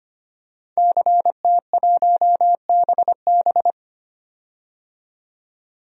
Morse code, heard as a keyed tone, about 25 wpm, 700 Hz. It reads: CT1BB
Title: DXCC-100 Call Signs - 5 Characters - 25wpm